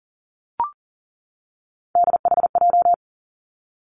Morse code, heard as a keyed tone, about 40 wpm, 700 Hz. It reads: DHJ
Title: Three Challenging Letter-Number Combinations 40wpm